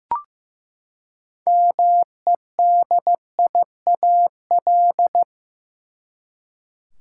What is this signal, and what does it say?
Morse code, heard as a keyed tone, about 15 wpm, 700 Hz. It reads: MEDIAL
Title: Practice QSO - Exchange 3 - More Info - 15wpm